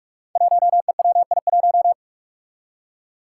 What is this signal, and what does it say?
Morse code, heard as a keyed tone, about 44 wpm, 700 Hz. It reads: 1EWI1
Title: Five Challenging Letter-Number Combinations 45wpm